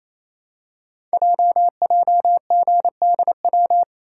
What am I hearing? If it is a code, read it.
Morse code, heard as a keyed tone, about 27 wpm, 700 Hz. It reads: JJGDW
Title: Code Groups - 5 Letters - 28wpm